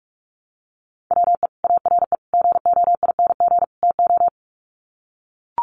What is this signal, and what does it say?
Morse code, heard as a keyed tone, about 44 wpm, 700 Hz. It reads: WE ARE GOING TO
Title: Sets of 4 Words 45wpm